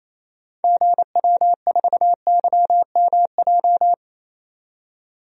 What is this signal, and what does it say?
Morse code, heard as a keyed tone, about 28 wpm, 700 Hz. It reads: GW4YMJ
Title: DXCC-100 Call Signs - 6 Characters - 28wpm